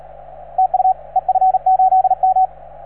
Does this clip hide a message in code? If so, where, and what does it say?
Yes: Morse code from 0.6 to 2.5 s, heard as a keyed tone, about 38 wpm, 750 Hz, with radio noise. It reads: TU EF8M